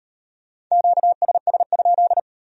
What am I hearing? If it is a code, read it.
Morse code, heard as a keyed tone, about 36 wpm, 700 Hz. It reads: QSS?